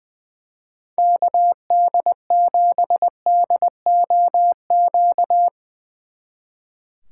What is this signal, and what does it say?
Morse code, heard as a keyed tone, about 23 wpm, 700 Hz. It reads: KD7DOQ
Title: Speed Racing - USA 2x3 Call Signs 20wpm